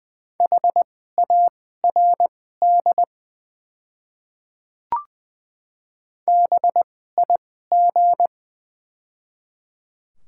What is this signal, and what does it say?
Morse code, heard as a keyed tone, about 20 wpm, spaced as for 14 wpm, 690 Hz. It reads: HARD BIG
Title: 2x Character Spacing - Top 300 Words - 20wpm